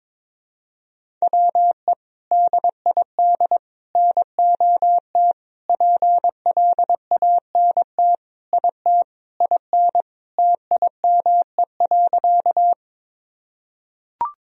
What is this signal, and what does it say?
Morse code, heard as a keyed tone, about 22 wpm, 700 Hz. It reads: WE DID NOT PLANT IT IN TIME.
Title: Sentences from Top 1000 Words - Review Part 1 - 22wpm